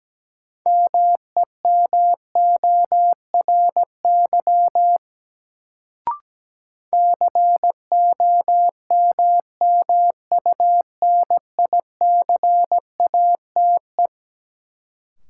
Morse code, heard as a keyed tone, about 17 wpm, 700 Hz. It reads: MEMORY COMMUNICATE
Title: Sentences from Top 1300 Words - New Word Drill - 17wpm